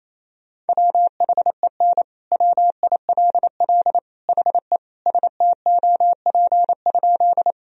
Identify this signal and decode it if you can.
Morse code, heard as a keyed tone, about 28 wpm, 700 Hz. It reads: WHEN WILL HE STOP?
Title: Sentences from Top 1000 Words - Review Part 1 - 28wpm